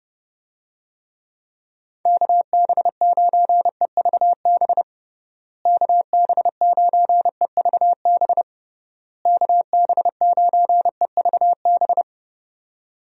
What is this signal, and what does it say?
Morse code, heard as a keyed tone, about 30 wpm, 700 Hz. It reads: KB9EVB KB9EVB KB9EVB
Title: USA Call Signs - 6 Characters - Repeated 3x - 30wpm